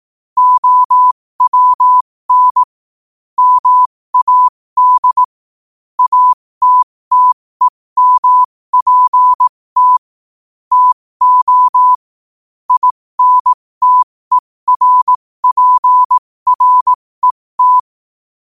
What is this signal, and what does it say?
Morse code, heard as a keyed tone, about 18 wpm, spaced as for 15 wpm, 1000 Hz. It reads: OWN MAD ATTEMPT TO INTERPRET
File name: qotdfiles / 15wpm